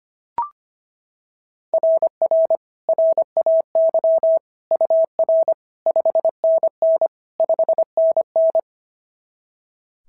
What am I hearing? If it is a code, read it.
Morse code, heard as a keyed tone, about 22 wpm, 650 Hz. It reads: RR RAY UR 5NN 5NN
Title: SOTA QSO Practice 25wpm